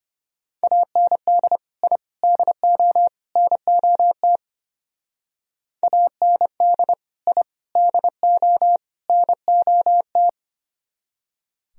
Morse code, heard as a keyed tone, about 32 wpm, 700 Hz. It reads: AND I DO NOT AND I DO NOT